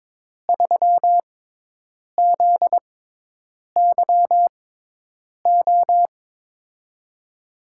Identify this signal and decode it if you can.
Morse code, heard as a keyed tone, about 21 wpm, spaced as for 8 wpm, 700 Hz. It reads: 3ZYO
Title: USA Call Signs - 6x Character Spacing - 22wpm